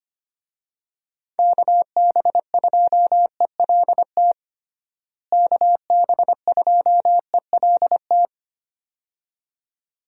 Morse code, heard as a keyed tone, about 25 wpm, 700 Hz. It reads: KB2ELT KB2ELT
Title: USA Call Signs - 6 Characters - Repeated 2x - 25wpm